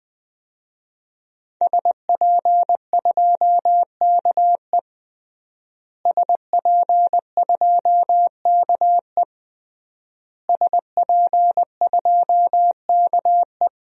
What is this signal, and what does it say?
Morse code, heard as a keyed tone, about 21 wpm, 700 Hz. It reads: SP2KE SP2KE SP2KE
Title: DXCC-100 Call Signs - 5 Characters - Repeated 3x - 20wpm